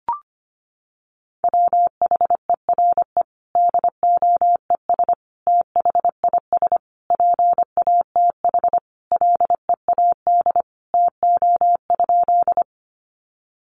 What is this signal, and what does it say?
Morse code, heard as a keyed tone, about 25 wpm, 700 Hz. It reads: WHERE DOES THIS PATH LEAD TO?
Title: Sentences from Top 1400 Words 25wpm